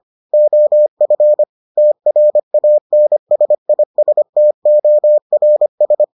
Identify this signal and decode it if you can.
Morse code, heard as a keyed tone, about 25 wpm, 600 Hz. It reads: OF TRANSISTORS